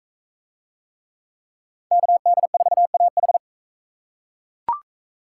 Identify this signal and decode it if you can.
Morse code, heard as a keyed tone, about 39 wpm, 700 Hz. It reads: KD4AH